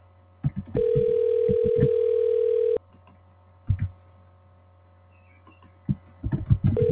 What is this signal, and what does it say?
Ringback tone (North America), 440 + 480 Hz.